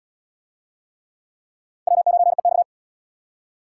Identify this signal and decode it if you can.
Morse code, heard as a keyed tone, about 70 wpm, 700 Hz. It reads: W9EX